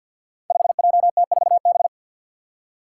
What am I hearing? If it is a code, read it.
Morse code, heard as a keyed tone, about 51 wpm, 700 Hz. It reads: 5JT4B
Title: Five Challenging Letter-Number Combinations 50wpm